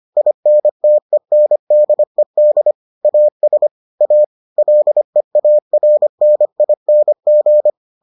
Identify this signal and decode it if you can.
Morse code, heard as a keyed tone, about 25 wpm, 600 Hz. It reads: INTENDED AS A LEARNING